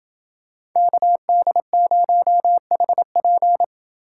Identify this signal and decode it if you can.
Morse code, heard as a keyed tone, about 28 wpm, 700 Hz. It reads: KD0HP